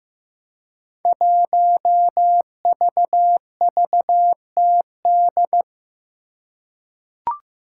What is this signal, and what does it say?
Morse code, heard as a keyed tone, about 15 wpm, 700 Hz. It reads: 1VVTD
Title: Five Challenging Letter-Number Combinations 15wpm